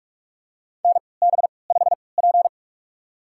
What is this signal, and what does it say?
Morse code, heard as a keyed tone, about 46 wpm, spaced as for 26 wpm, 690 Hz. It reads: NB5P
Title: USA Call Signs - 4 Characters - 3x Character Spacing - 45wpm